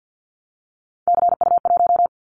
Morse code, heard as a keyed tone, about 50 wpm, 700 Hz. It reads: CU1
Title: Callsign Prefixes for Top 50 DXCC - 50wpm